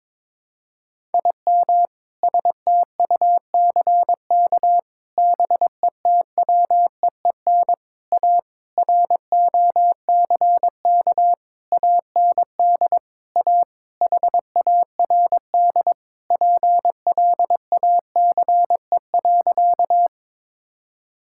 Morse code, heard as a keyed tone, about 21 wpm, 700 Hz. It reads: IM STUCK BETWEEN A ROCK AND A HARD PLACE.
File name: Sentences with 7-Word Phrases - 22wpm